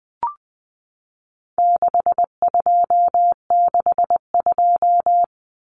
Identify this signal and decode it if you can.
Morse code, heard as a keyed tone, about 20 wpm, 700 Hz. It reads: 6262
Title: Numbers - 4-Digit 20wpm